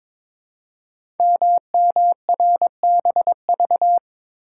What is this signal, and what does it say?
Morse code, heard as a keyed tone, about 22 wpm, 700 Hz. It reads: MMRBV
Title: Code Groups - 5 Letters - 22wpm